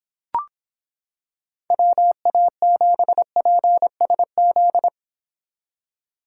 Morse code, heard as a keyed tone, about 26 wpm, 700 Hz. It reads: WA7PSZ